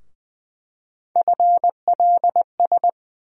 Morse code, heard as a keyed tone, about 20 wpm, 700 Hz. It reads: FLS
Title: USA Call Sign Postfixes - 3 Letters 20wpm